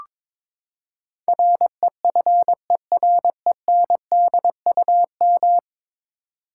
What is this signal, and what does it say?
Morse code, heard as a keyed tone, about 22 wpm, 700 Hz. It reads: REFERENDUM
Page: morse-code-ninja-practice-22wpm